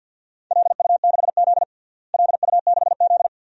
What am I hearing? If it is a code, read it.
Morse code, heard as a keyed tone, about 49 wpm, 700 Hz. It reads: LU6Z LU6Z
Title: DXCC-100 Call Signs - 4 Characters - Repeated 2x - 50wpm